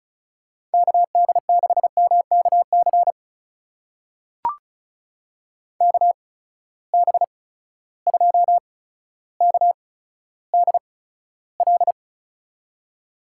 Morse code, heard as a keyed tone, about 36 wpm, spaced as for 15 wpm, 700 Hz. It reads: KD6MKC KB2KDL
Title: USA Call Signs - 6 Characters - 8x Character Spacing - 35wpm